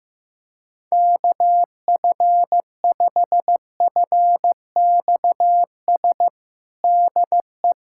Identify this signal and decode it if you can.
Morse code, heard as a keyed tone, about 15 wpm, 700 Hz. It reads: KF5FXS DE
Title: USA Callsign DE Callsign - 6 Characters - 15wpm